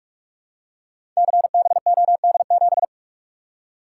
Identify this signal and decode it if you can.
Morse code, heard as a keyed tone, about 44 wpm, 700 Hz. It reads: CBOD7
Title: Code Groups - 5 Letter:Numbers - 45wpm